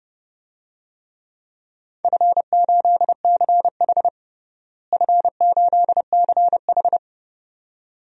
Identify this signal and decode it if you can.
Morse code, heard as a keyed tone, about 30 wpm, 700 Hz. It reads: F8CH F8CH
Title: DXCC-100 Call Signs - 4 Characters - Repeated 2x - 30wpm